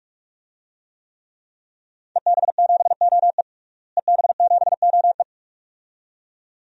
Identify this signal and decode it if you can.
Morse code, heard as a keyed tone, about 43 wpm, 700 Hz. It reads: EB7OE EB7OE
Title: DXCC-100 Call Signs - 5 Characters - Repeated 2x - 45wpm